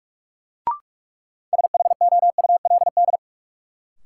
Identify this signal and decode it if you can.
Morse code, heard as a keyed tone, about 47 wpm, 700 Hz. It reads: SHOULD
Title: Sentences from Top 300 Words - New Word Drill - 45wpm